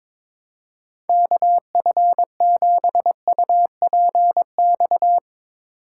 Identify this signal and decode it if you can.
Morse code, heard as a keyed tone, about 22 wpm, 700 Hz. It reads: KF7UPX
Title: USA Call Signs - 6 Characters - 2x Character Spacing - 22wpm